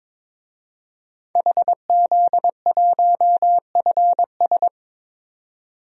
Morse code, heard as a keyed tone, about 21 wpm, 700 Hz. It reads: HZ1FS
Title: DXCC-100 Call Signs - 5 Characters - 22wpm